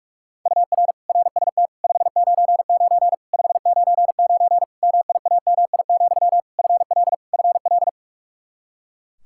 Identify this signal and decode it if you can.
Morse code, heard as a keyed tone, about 41 wpm, 700 Hz. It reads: UR RST 599 599 MIAMI, FL FL
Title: Practice QSO - Exchange 1 - Beginning - 45wpm